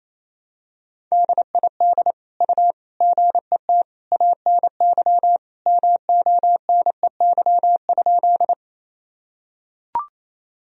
Morse code, heard as a keyed tone, about 27 wpm, 700 Hz. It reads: DID U GET ANY MONEY?
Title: Sentences from Top 400 Words and Abbreviations 28wpm